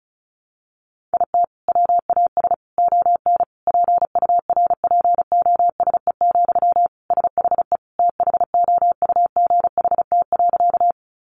Morse code, heard as a keyed tone, about 34 wpm, 700 Hz. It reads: IT WAS ON PURPOSE, SHE THOUGHT.
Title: Sentences from Top 800 Words 35wpm